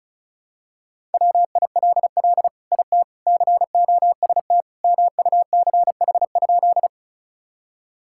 Morse code, heard as a keyed tone, about 36 wpm, 700 Hz. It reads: WILL IT COST MUCH?
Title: Sentences from Top 1000 Words - Review Part 5 - 35wpm